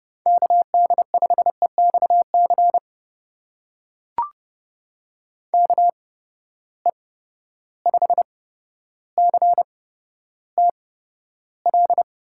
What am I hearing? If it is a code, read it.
Morse code, heard as a keyed tone, about 30 wpm, spaced as for 13 wpm, 700 Hz. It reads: KD5EXC KE5CTL
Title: USA Call Signs - 6 Characters - 8x Character Spacing - 30wpm